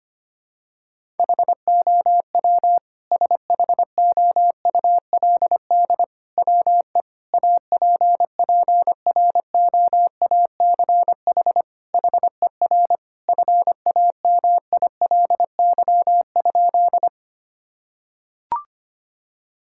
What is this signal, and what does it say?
Morse code, heard as a keyed tone, about 25 wpm, 700 Hz. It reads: HOW SHOULD WE APPROACH HER FAMILY?
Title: Sentences from Top 1000 Words - Review Part 6 - 25wpm